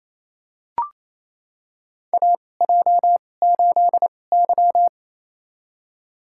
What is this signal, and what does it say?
Morse code, heard as a keyed tone, about 28 wpm, spaced as for 20 wpm, 700 Hz. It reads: AJ8Y
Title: USA Call Signs - 2x Character Spacing - 28wpm